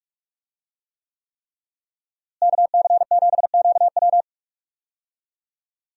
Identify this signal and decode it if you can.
Morse code, heard as a keyed tone, about 46 wpm, 700 Hz. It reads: KC7QW
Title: USA Call Signs - 5 Characters - Repeated 3x - 45wpm